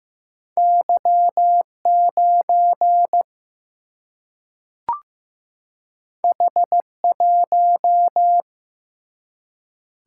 Morse code, heard as a keyed tone, about 15 wpm, 700 Hz. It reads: Y 9 H 1
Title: Single-Letter Single-Number 15wpm